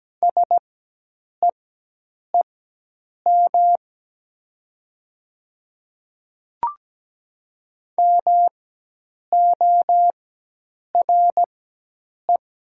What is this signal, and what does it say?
Morse code, heard as a keyed tone, about 17 wpm, spaced as for 8 wpm, 700 Hz. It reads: SEEM MORE
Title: 4x Character Spacing - Top 300 Words - 17wpm